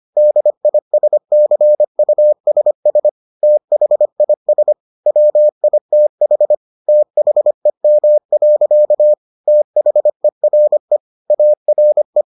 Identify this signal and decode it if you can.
Morse code, heard as a keyed tone, about 25 wpm, 600 Hz. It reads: DISCUSS THIS WITH THEM. THERE ARE